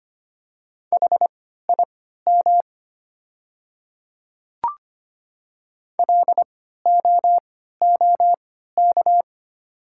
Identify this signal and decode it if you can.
Morse code, heard as a keyed tone, about 25 wpm, spaced as for 14 wpm, 690 Hz. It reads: HIM LOOK
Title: 3x Character Spacing - Top 100 Words - 25wpm